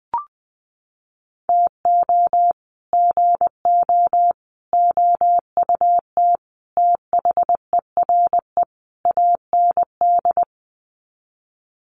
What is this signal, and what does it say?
Morse code, heard as a keyed tone, about 20 wpm, 700 Hz. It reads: TO GO OUT THERE AND
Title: Sets of 5 Words Encore2 20wpm